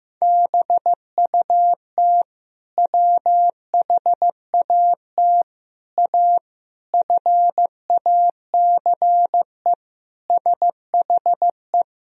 Morse code, heard as a keyed tone, about 16 wpm, 700 Hz. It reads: BUT WHAT A FACE SHE
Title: Sentences from Top 1000 Words - Review Part 2 - 15wpm